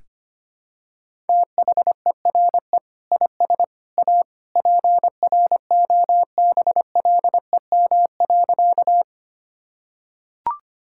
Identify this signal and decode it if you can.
Morse code, heard as a keyed tone, about 25 wpm, 700 Hz. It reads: THERE IS A PROBLEM.
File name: practice / 25wpm / Sentences with 3-Word Phrases - 25wpm